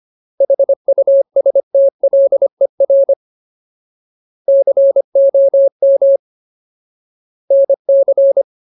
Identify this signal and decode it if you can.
Morse code, heard as a keyed tone, about 25 wpm, 550 Hz. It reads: HUSTLER COM NC
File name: Ultimate Word List / 25WPM Ultimate